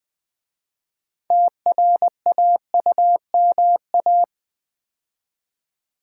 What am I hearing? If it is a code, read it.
Morse code, heard as a keyed tone, about 20 wpm, 700 Hz. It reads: TRAUMA
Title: 6 Letter Words - First 3 Letters Given - 20wpm